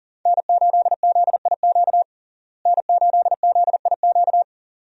Morse code, heard as a keyed tone, about 40 wpm, 700 Hz. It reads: N8ZIQ N8ZIQ
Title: USA Call Signs - 5 Characters - Repeated 2x - 40wpm